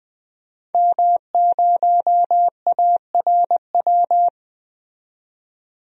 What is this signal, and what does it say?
Morse code, heard as a keyed tone, about 20 wpm, 700 Hz. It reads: M0ARW
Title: DXCC-100 Call Signs - 5 Characters - 20wpm